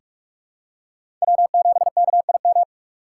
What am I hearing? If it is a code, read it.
Morse code, heard as a keyed tone, about 45 wpm, 700 Hz. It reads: W7KIM
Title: USA Call Signs - 5 Characters - 2x Character Spacing - 45wpm